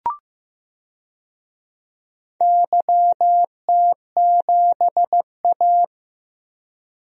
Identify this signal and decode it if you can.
Morse code, heard as a keyed tone, about 15 wpm, 700 Hz. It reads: YT7A
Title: DXCC-100 Call Signs - 4 Characters - Repeated 3x - 15wpm